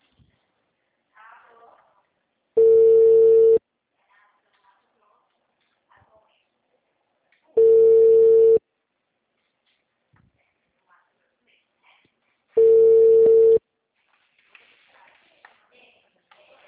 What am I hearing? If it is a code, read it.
Ringback tone (China, 450 Hz).
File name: out- -2002- - - .19563